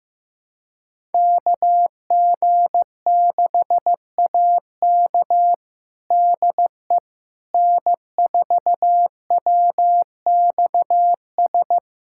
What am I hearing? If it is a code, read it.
Morse code, heard as a keyed tone, about 15 wpm, 700 Hz. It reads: KG6AK DE N4WXS
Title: USA Callsign DE Callsign - 5 Characters - 15wpm